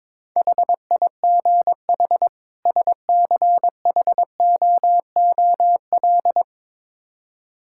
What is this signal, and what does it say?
Morse code, heard as a keyed tone, about 22 wpm, 700 Hz. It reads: HIGH SCHOOL
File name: Build A Phrase - 2-Word Phrases - 22wpm